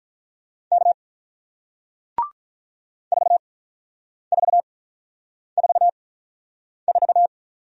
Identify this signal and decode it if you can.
Morse code, heard as a keyed tone, about 46 wpm, spaced as for 9 wpm, 700 Hz. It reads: K 4444